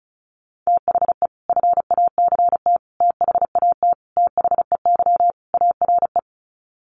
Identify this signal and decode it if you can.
Morse code, heard as a keyed tone, about 34 wpm, 700 Hz. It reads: THE FACT THAT THEY ARE
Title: Sets of 5 Words Encore2 35wpm